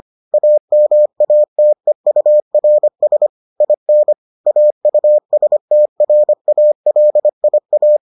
Morse code, heard as a keyed tone, about 25 wpm, 600 Hz. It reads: AMATEURS IN AUSTRALIA